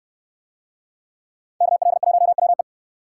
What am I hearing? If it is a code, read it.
Morse code, heard as a keyed tone, about 69 wpm, 700 Hz. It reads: KL1WE